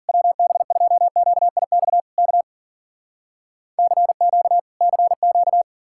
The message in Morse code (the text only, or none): WB2QIX K CQ CQ